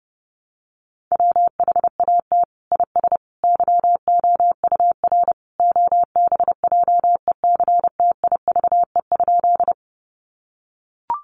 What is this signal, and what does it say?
Morse code, heard as a keyed tone, about 30 wpm, 700 Hz. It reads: WHAT IS YOUR OBJECTIVE?
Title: Sentences from Top 1200 Words 30wpm